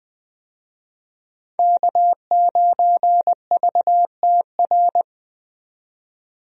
Morse code, heard as a keyed tone, about 20 wpm, 700 Hz. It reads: K9VTR
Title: USA Call Signs - 5 Characters - 20wpm